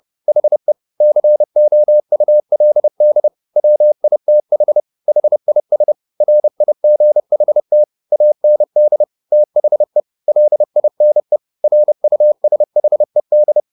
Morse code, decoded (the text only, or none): HE COULD WITH HIS RIGHT AND THE LINE RUSHED